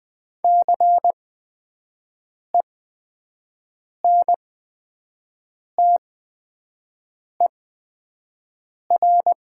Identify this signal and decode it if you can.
Morse code, heard as a keyed tone, about 20 wpm, spaced as for 5 wpm, 700 Hz. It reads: CENTER
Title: 8x Character Spacing - Top 500 Words - 20wpm